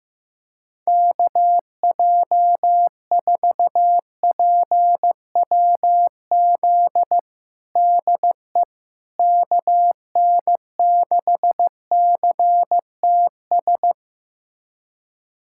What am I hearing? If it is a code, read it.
Morse code, heard as a keyed tone, about 15 wpm, 700 Hz. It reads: KJ4PWZ DE KN6CTS
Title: USA Callsign DE Callsign 15wpm